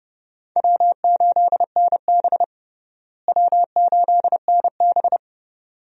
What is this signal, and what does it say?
Morse code, heard as a keyed tone, about 30 wpm, 700 Hz. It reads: W8NB W8NB
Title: USA Call Signs - 4 Characters - Repeated 2x - 30wpm